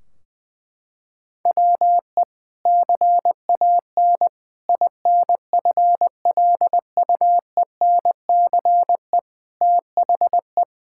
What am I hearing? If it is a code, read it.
Morse code, heard as a keyed tone, about 21 wpm, 700 Hz. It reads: WE CAN INFLUENCE THE
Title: Sentences from Top 1600 Words 20wpm